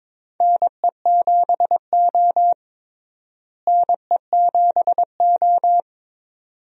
Morse code, heard as a keyed tone, about 22 wpm, 700 Hz. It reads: NE7O NE7O
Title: USA Call Signs - 4 Characters - Repeated 2x - 22wpm